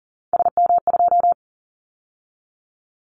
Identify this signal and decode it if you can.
Morse code, heard as a keyed tone, about 37 wpm, 700 Hz. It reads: SM2
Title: Callsign Prefixes for Top 25 DXCC - 40wpm